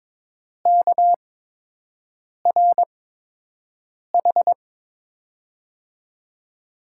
Morse code, heard as a keyed tone, about 22 wpm, spaced as for 6 wpm, 700 Hz. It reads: KRH
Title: USA Call Signs - 8x Character Spacing - 22wpm